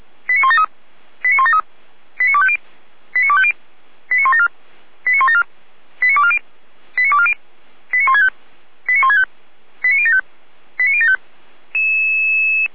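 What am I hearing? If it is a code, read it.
Five-tone selective call (ZVEI), heard as quick runs of five tones, tones of about 70 ms. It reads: 89173 89173 89250 89250 89174 89174 89240 89240 89165 89165 89075 89075
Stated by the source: ZVEI five-tone call 89336 received by SDR.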